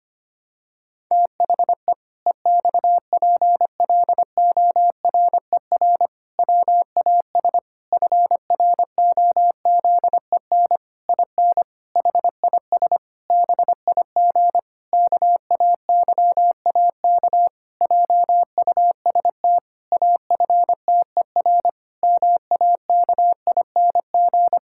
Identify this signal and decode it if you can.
Morse code, heard as a keyed tone, about 25 wpm, 700 Hz. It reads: THE EXPLORER WAS FROZEN IN HIS BIG KAYAK JUST AFTER MAKING
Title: Pangram Challenge v2 - 25wpm